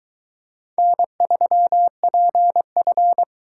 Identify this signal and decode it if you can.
Morse code, heard as a keyed tone, about 22 wpm, 700 Hz. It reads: N3PF